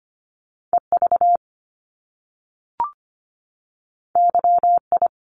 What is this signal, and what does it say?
Morse code, heard as a keyed tone, about 25 wpm, 700 Hz. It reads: E V Y I
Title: Top Two Letter Combinations in Words 25wpm